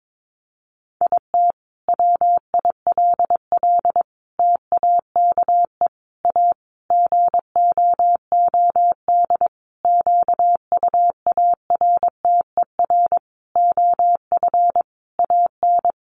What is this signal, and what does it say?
Morse code, heard as a keyed tone, about 22 wpm, 700 Hz. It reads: IT WILL TAKE A GOOD QUARTER OF AN
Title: Sentences from Top 1300 Words 22wpm